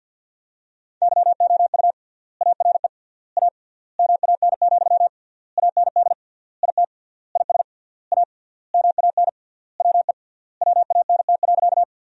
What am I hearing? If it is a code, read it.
Morse code, heard as a keyed tone, about 47 wpm, 700 Hz. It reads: YOU ARE A MAN, AND IT IS A MAN WE WANT.
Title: 3x Word-Spacing - Sentences from Top 200 Words - 50wpm